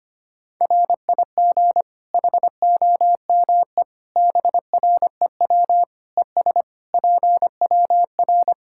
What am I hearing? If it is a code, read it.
Morse code, heard as a keyed tone, about 25 wpm, 700 Hz. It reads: RIG HOME BREW ES PWR
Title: Practice QSO - Exchange 2 - Info - 25wpm